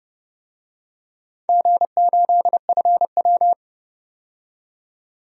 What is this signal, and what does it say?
Morse code, heard as a keyed tone, about 29 wpm, 700 Hz. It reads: G8FW
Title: DXCC-100 Call Signs - 4 Characters - 30wpm